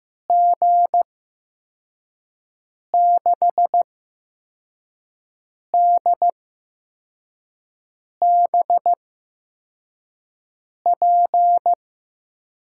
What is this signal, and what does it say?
Morse code, heard as a keyed tone, about 15 wpm, spaced as for 4 wpm, 700 Hz. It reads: G6DBP
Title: USA Call Signs - 6 Characters - 8x Character Spacing - 15wpm